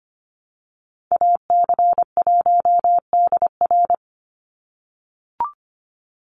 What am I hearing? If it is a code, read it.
Morse code, heard as a keyed tone, about 25 wpm, 700 Hz. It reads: AC1DR
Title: USA Call Signs 25wpm